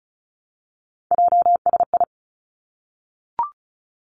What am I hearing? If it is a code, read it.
Morse code, heard as a keyed tone, about 36 wpm, 700 Hz. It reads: JSI
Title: Three Challenging Letter-Number Combinations 35wpm